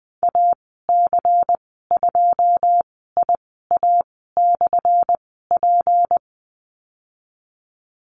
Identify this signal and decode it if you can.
Morse code, heard as a keyed tone, about 19 wpm, spaced as for 14 wpm, 700 Hz. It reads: AC2IA/P
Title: 179 Lesson 29g - Slash - Call Signs 20wpm